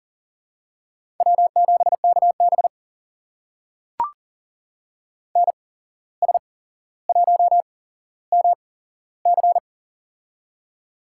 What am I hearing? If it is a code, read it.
Morse code, heard as a keyed tone, about 38 wpm, spaced as for 16 wpm, 700 Hz. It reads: W7KB NS1MC
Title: USA Call Signs - 8x Character Spacing - 40wpm